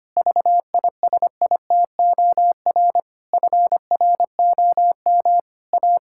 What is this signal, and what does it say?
Morse code, heard as a keyed tone, about 25 wpm, 700 Hz. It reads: VISITOR FROM A
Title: Pangram Challenge v2 - 25wpm